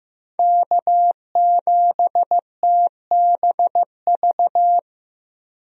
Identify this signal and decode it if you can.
Morse code, heard as a keyed tone, about 15 wpm, 700 Hz. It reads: K7TBV
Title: USA Call Signs - 5 Characters - 15wpm